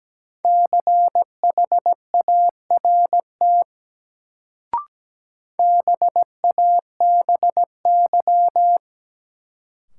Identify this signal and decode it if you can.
Morse code, heard as a keyed tone, about 17 wpm, 700 Hz. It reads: CHART BABY
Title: Sentences from Top 1300 Words - New Word Drill - 17wpm